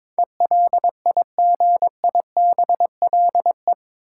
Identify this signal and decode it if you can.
Morse code, heard as a keyed tone, about 22 wpm, 700 Hz. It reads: ELIGIBLE
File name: practice / 22wpm / Eight Letter Words - 22wpm